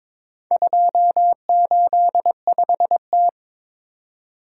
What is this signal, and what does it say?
Morse code, heard as a keyed tone, about 22 wpm, 700 Hz. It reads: 285T
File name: practice / 22wpm / Four Letter-Number Combinations - 22wpm